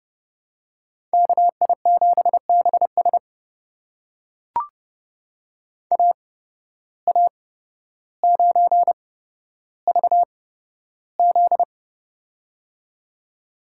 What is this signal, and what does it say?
Morse code, heard as a keyed tone, about 30 wpm, spaced as for 13 wpm, 700 Hz. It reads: KI7BS AA9VZ
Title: USA Call Signs - 5 Characters - 8x Character Spacing - 30wpm